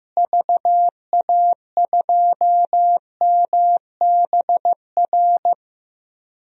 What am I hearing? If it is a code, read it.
Morse code, heard as a keyed tone, about 16 wpm, 700 Hz. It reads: VA2MBR
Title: DXCC-100 Call Signs - 6 Characters - Repeated 3x - 15wpm